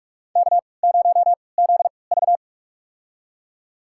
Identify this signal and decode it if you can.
Morse code, heard as a keyed tone, about 46 wpm, spaced as for 26 wpm, 700 Hz. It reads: K0ZV
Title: USA Call Signs - 4 Characters - 3x Character Spacing - 45wpm